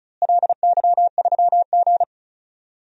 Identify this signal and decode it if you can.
Morse code, heard as a keyed tone, about 34 wpm, 700 Hz. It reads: LY3G
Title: DXCC-100 Call Signs - 4 Characters - Repeated 2x - 35wpm